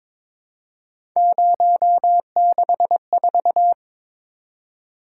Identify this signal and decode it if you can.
Morse code, heard as a keyed tone, about 21 wpm, 700 Hz. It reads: 064
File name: practice / 22wpm / Three Letter-Number Combinations - 22wpm